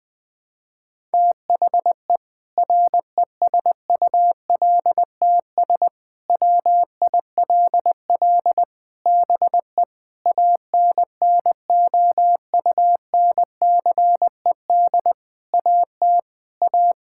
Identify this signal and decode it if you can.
Morse code, heard as a keyed tone, about 20 wpm, 700 Hz. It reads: THE RESULTS WILL BE ANNOUNCED AT A
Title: Sentences with 7-Word Phrases - Spoken Beginning - 20wpm